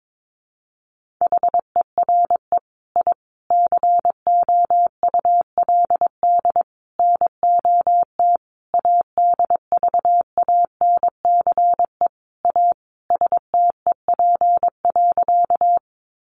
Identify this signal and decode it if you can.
Morse code, heard as a keyed tone, about 22 wpm, 700 Hz. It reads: HERE I COULD NOT ADVANCE A STEP.
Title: Sentences from Top 1400 Words 22wpm